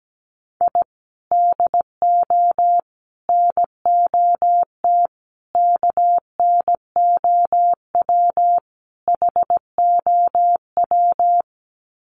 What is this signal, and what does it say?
Morse code, heard as a keyed tone, about 17 wpm, 700 Hz. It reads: I DO NOT KNOW HOW
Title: Sets of 5 Words 17wpm